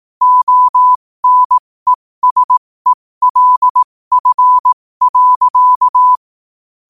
Morse code, heard as a keyed tone, about 18 wpm, spaced as for 15 wpm, 1000 Hz. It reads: ONESELF.
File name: qotdfiles / 15wpm